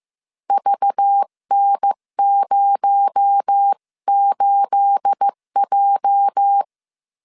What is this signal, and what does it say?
Morse code, heard as a keyed tone, about 15 wpm, 800 Hz. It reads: VN08J
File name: CW / DET / 1 9 1500C S B 14wpm 3302